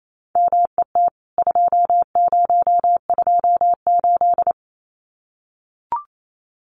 Morse code, heard as a keyed tone, about 28 wpm, 700 Hz. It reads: MET 2028
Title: CWT Exchange 28wpm